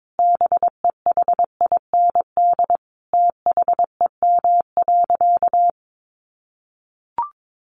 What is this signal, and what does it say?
Morse code, heard as a keyed tone, about 22 wpm, 700 Hz. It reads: BEHIND THEM.
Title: Sentences from Top 1400 Words 22wpm